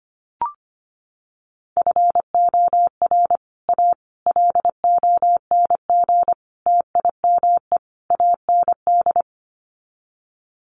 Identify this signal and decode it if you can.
Morse code, heard as a keyed tone, about 25 wpm, 700 Hz. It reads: FOR A LONG TIME AND
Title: Sets of 5 Words Encore2 25wpm